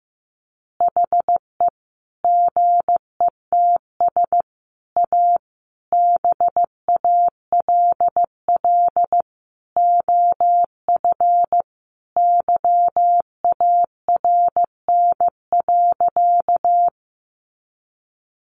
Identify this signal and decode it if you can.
Morse code, heard as a keyed tone, about 15 wpm, 700 Hz. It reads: HE GETS A BALL OF YARN.
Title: Additional Easy Phrases 15wpm